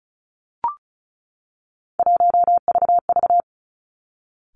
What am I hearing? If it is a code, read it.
Morse code, heard as a keyed tone, about 36 wpm, 700 Hz. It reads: 1VV
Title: Three Challenging Letter-Number Combinations 35wpm